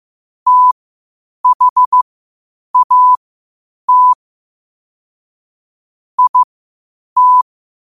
Morse code, heard as a keyed tone, about 15 wpm, spaced as for 8 wpm, 1000 Hz. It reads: THAT IT
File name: qotdfiles / 8wpm